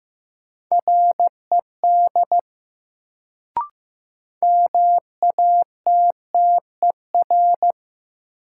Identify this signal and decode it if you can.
Morse code, heard as a keyed tone, about 15 wpm, 700 Hz. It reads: RED MATTER
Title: Sentences from Top 700 Words - New Word Drill - 15wpm